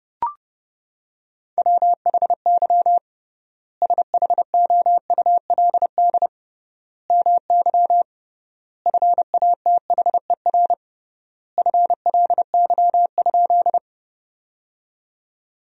Morse code, heard as a keyed tone, about 30 wpm, 700 Hz. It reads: WHY SHOULD MY FATHER FLY?
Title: 3x Word-Spacing - Sentences from Top 500 Words - 30wpm